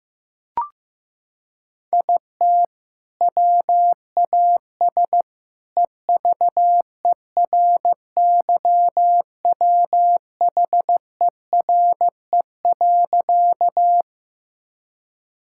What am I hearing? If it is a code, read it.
Morse code, heard as a keyed tone, about 15 wpm, 700 Hz. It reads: IT WAS EVERYWHERE.
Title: Sentences from Top 1600 Words 15wpm